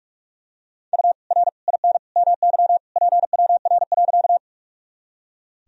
Morse code, heard as a keyed tone, about 41 wpm, 700 Hz. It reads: U R IN MY PWR.
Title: Sentences from Top 500 Words and Abbreviations 45wpm